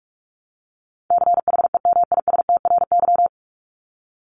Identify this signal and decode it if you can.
Morse code, heard as a keyed tone, about 44 wpm, 700 Hz. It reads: CHEMISTRY
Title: Top 751-1000 Words 45wpm